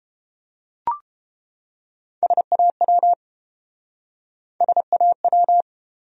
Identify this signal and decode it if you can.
Morse code, heard as a keyed tone, about 29 wpm, 700 Hz. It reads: SAW SAW